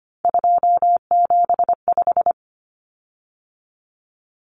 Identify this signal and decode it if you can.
Morse code, heard as a keyed tone, about 25 wpm, 700 Hz. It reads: 275
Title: Common Cut Numbers - 3-Digit 25wpm